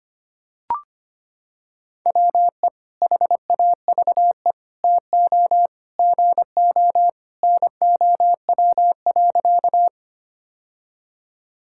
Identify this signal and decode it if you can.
Morse code, heard as a keyed tone, about 25 wpm, 700 Hz. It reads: WE HAVE TO GO NOW.
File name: practice / 25wpm / Sentences with 3-Word Phrases - 25wpm